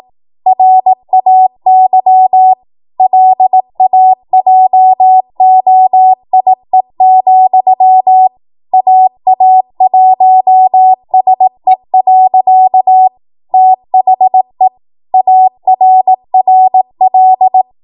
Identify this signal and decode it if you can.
Morse code, heard as a keyed tone, about 18 wpm, 750 Hz. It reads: RAY LAJOIE, AA1SE. THE ARRL
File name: Morse / 18WPM